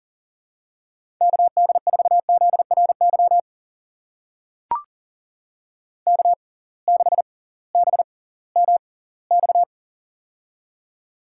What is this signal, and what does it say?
Morse code, heard as a keyed tone, about 39 wpm, spaced as for 21 wpm, 700 Hz. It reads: KD4ZRY K6BMX
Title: USA Call Signs - 6x Character Spacing - 40wpm